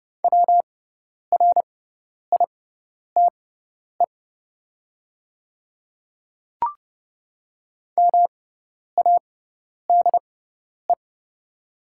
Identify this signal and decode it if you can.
Morse code, heard as a keyed tone, about 30 wpm, spaced as for 10 wpm, 700 Hz. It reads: WRITE MADE
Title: 6x Character Spacing - Top 300 Words - 30wpm